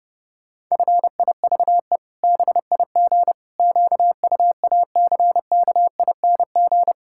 Morse code, heard as a keyed tone, about 32 wpm, 700 Hz. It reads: FIVE BIG QUACKING
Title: Pangram Challenge v2 - 30wpm